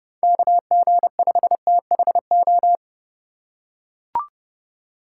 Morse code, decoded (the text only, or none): KG5THO